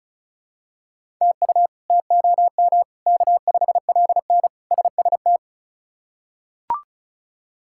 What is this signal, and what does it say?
Morse code, heard as a keyed tone, about 36 wpm, 700 Hz. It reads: TU TOM K5LN SST
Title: K1USN SST QSO Practice - 35wpm